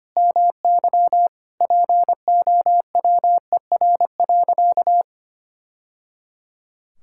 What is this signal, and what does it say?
Morse code, heard as a keyed tone, about 25 wpm, 700 Hz. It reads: MY POWER.
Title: Sentences from Top 1000 Words - Review Part 6 - 25wpm